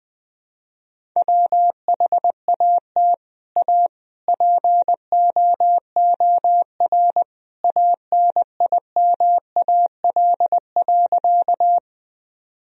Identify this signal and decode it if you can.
Morse code, heard as a keyed tone, about 20 wpm, 700 Hz. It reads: WHAT A POOR ANIMAL.
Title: Sentences from Top 1000 Words - Review Part 1 - 20wpm